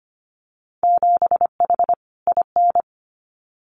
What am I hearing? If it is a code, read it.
Morse code, heard as a keyed tone, about 25 wpm, 700 Hz. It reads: 7H IN
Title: Winter Field Day Exchanges Encore 25wpm